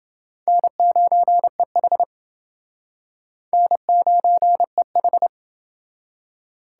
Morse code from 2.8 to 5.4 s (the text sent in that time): N9EH